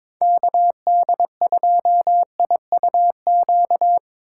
Morse code, heard as a keyed tone, about 24 wpm, 700 Hz. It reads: KD2IUQ
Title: Speed Racing - USA 2x3 Call Signs 22wpm